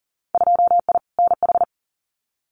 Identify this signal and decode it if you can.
Morse code, heard as a keyed tone, about 42 wpm, 700 Hz. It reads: 2I NH
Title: Winter Field Day Exchanges Encore 40wpm